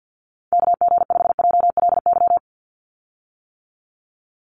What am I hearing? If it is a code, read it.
Morse code, heard as a keyed tone, about 50 wpm, 700 Hz. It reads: KG5JLY
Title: USA Call Signs 50wpm